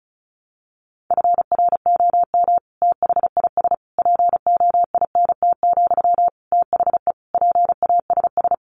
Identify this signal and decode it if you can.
Morse code, heard as a keyed tone, about 34 wpm, 700 Hz. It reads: FROM THIS POINT, THE PASS
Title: Sentences from Top 1200 Words 35wpm